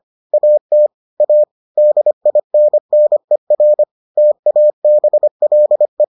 Morse code, heard as a keyed tone, about 25 wpm, 600 Hz. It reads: AT A DINNER TABLE